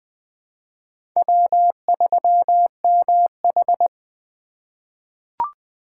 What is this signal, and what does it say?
Morse code, heard as a keyed tone, about 20 wpm, 700 Hz. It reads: W3MH
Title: USA Call Signs - 4 Characters - 3x Character Spacing - 20wpm